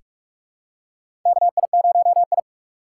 Morse code, heard as a keyed tone, about 44 wpm, 700 Hz. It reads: KI0I
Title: K1USN SST QSO Practice - 45wpm